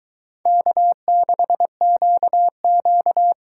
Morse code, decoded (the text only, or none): K6QQ